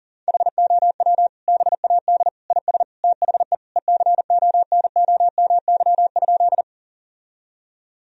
Morse code, heard as a keyed tone, about 40 wpm, 700 Hz. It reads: HOW BAD IS THE ECONOMY?
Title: Sentences from Top 1000 Words - Review Part 3 - 40wpm